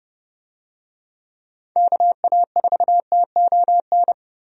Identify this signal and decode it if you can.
Morse code, heard as a keyed tone, about 30 wpm, 700 Hz. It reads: KA4TON
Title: USA Call Signs - 6 Characters - 30wpm